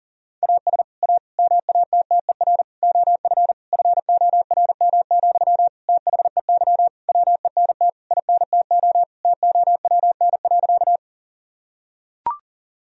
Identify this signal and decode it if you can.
Morse code, heard as a keyed tone, about 40 wpm, 700 Hz. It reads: AS A MATTER OF FORM, THEY WENT INTO TOWN.
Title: Sentences from Top 1000 Words - Review Part 1 - 40wpm